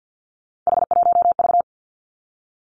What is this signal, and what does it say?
Morse code, heard as a keyed tone, about 50 wpm, 700 Hz. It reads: H1V
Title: Three Challenging Letter-Number Combinations 50wpm